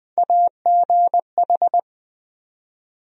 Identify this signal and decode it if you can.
Morse code, heard as a keyed tone, about 20 wpm, 700 Hz. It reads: AGH
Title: USA Call Sign Postfixes - 3 Letters 20wpm